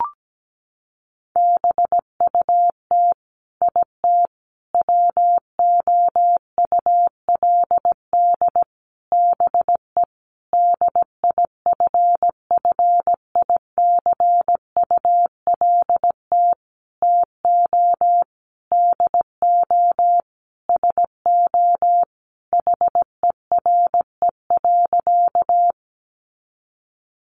Morse code, heard as a keyed tone, about 17 wpm, 700 Hz. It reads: BUT IT WOULD BE DIFFICULT TO DO SO HERE.
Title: Sentences from Top 600 Words 17wpm